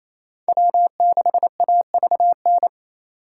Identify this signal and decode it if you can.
Morse code, heard as a keyed tone, about 35 wpm, 700 Hz. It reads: W6AVN